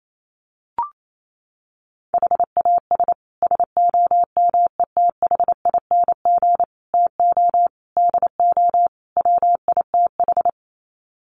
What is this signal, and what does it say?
Morse code, heard as a keyed tone, about 28 wpm, 700 Hz. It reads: HAS SOMETHING TO DO WITH
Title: Sets of 5 Words Encore 28wpm